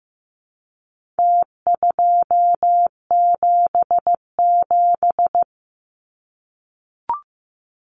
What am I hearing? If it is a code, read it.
Morse code, heard as a keyed tone, about 15 wpm, 700 Hz. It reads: T277
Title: Numbers - Contest Style - 4-Digit 15wpm